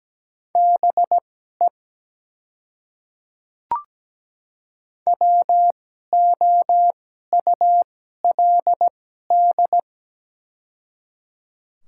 Morse code, heard as a keyed tone, about 17 wpm, spaced as for 12 wpm, 700 Hz. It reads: BE WOULD
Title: 2x Character Spacing - Top 100 Words - 17wpm